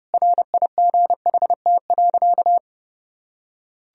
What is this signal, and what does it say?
Morse code, heard as a keyed tone, about 29 wpm, 700 Hz. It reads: RIGHT.
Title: Sentences from Top 500 Words and Abbreviations 30wpm